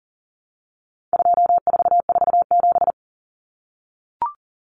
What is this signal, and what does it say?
Morse code, heard as a keyed tone, about 40 wpm, 700 Hz. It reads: 2447
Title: Numbers - 4-Digit 40wpm